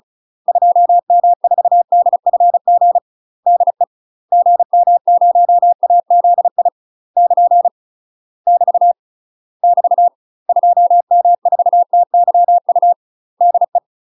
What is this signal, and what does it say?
Morse code, heard as a keyed tone, about 35 wpm, 700 Hz. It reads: 2M4DFG DE GM0AZI <KN> <BT> <BT> 2M4TYU DE
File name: UK MASTER QSOs TRAINING / 35WPM / UK MASTER QSOs 0039